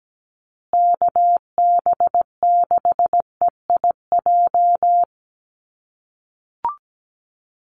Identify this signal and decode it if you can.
Morse code, heard as a keyed tone, about 17 wpm, 700 Hz. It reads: KB6EIJ
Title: USA Call Signs Repeated 2x 17wpm